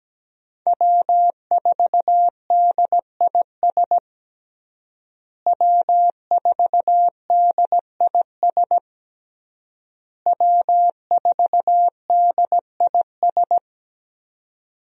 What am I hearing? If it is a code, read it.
Morse code, heard as a keyed tone, about 17 wpm, 700 Hz. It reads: W4DIS W4DIS W4DIS
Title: USA Call Signs - 5 Characters - Repeated 3x - 17wpm